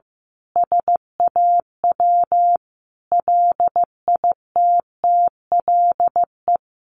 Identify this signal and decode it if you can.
Morse code, heard as a keyed tone, about 15 wpm, 700 Hz. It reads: SAW LITTLE
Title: Sentences from Top 1500 Words 15wpm